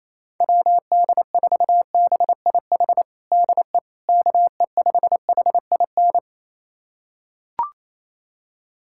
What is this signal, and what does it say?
Morse code, heard as a keyed tone, about 28 wpm, 700 Hz. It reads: WD4BIH DE KE5HIN
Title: USA Callsign DE Callsign 28wpm